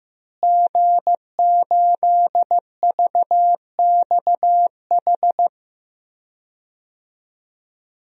Morse code, heard as a keyed tone, about 15 wpm, 700 Hz. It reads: G8VXH
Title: DXCC-100 Call Signs - 5 Characters - Repeated 2x - 15wpm